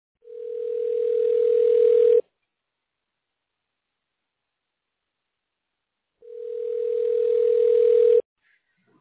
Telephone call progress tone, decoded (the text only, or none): ringback tone